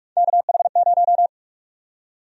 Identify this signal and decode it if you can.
Morse code, heard as a keyed tone, about 41 wpm, 700 Hz. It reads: KH0
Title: USA Call Sign Prefixes - 45wpm